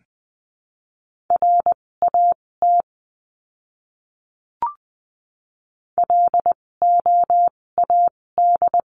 Morse code, heard as a keyed tone, about 20 wpm, spaced as for 16 wpm, 700 Hz. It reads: RAT LOAD
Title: 256 Lesson 42a - Words 12-20wpm